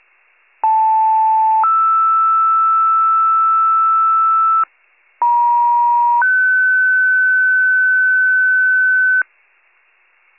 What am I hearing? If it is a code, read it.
Two-tone page, heard as a one-second tone, then a three-second tone, about 880 then 1320 Hz, about 950 then 1500 Hz. It is heard from 0.6 to 9.2 s.